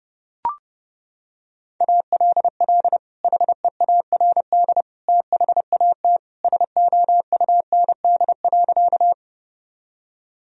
Morse code, heard as a keyed tone, about 30 wpm, 700 Hz. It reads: ALL HEARD THAT SOUND.
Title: Sentences from Top 1000 Words - Review Part 1 - 30wpm